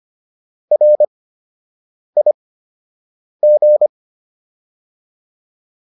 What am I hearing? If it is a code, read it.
Morse code, heard as a keyed tone, about 25 wpm, spaced as for 7 wpm, 600 Hz. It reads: RIG